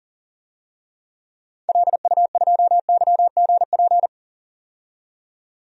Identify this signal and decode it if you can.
Morse code, heard as a keyed tone, about 40 wpm, 700 Hz. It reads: LU2YGP
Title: DXCC-100 Call Signs - 6 Characters - 40wpm